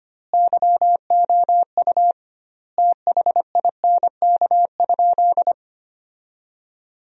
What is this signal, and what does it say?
Morse code, heard as a keyed tone, about 25 wpm, 700 Hz. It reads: YOU THINK?
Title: 2x Word-Spacing - Sentences from Top 200 Words - 25wpm